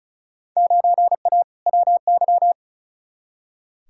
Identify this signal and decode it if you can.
Morse code, heard as a keyed tone, about 36 wpm, 700 Hz. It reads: 9A WY
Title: Field Day Exchanges Encore - 35wpm